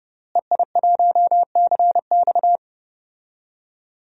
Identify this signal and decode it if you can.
Morse code, heard as a keyed tone, about 30 wpm, 700 Hz. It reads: EI1CX
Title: DXCC-100 Call Signs - 5 Characters - Repeated 2x - 30wpm